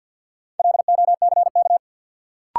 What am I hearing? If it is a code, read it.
Morse code, heard as a keyed tone, about 53 wpm, 700 Hz. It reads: LOCK
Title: Sentences from Top 1300 Words - New Word Drill Alt - 50wpm